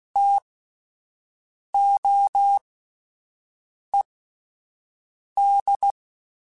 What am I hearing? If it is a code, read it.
Morse code, heard as a keyed tone, about 16 wpm, spaced as for 5 wpm, 800 Hz. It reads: TOED